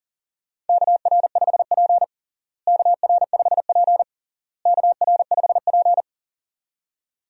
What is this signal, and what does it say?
Morse code, heard as a keyed tone, about 40 wpm, 700 Hz. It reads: KR5P KR5P KR5P
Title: USA Call Signs - 4 Characters - Repeated 3x - 40wpm